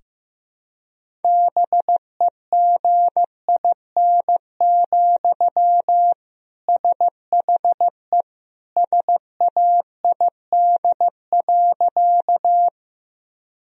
Morse code, heard as a keyed tone, about 15 wpm, 700 Hz. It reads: BEGIN, SHE SAID.
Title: Sentences from Top 1000 Words - Review Part 1 - 15wpm